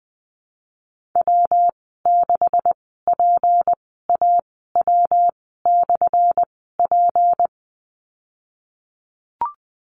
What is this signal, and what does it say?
Morse code, heard as a keyed tone, about 20 wpm, spaced as for 15 wpm, 700 Hz. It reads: W6PAW/P
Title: 179 Lesson 29g - Slash - Call Signs 20wpm